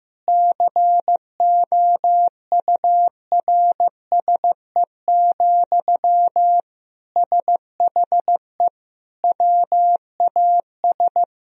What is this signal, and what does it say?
Morse code, heard as a keyed tone, about 15 wpm, 700 Hz. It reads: COURSE, SHE WAS
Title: Sentences from Top 1000 Words - Review Part 4 - 15wpm